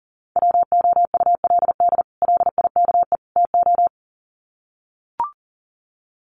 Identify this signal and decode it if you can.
Morse code, heard as a keyed tone, about 37 wpm, 700 Hz. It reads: WOULD LIKE TO
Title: Sets of 3 Words 40wpm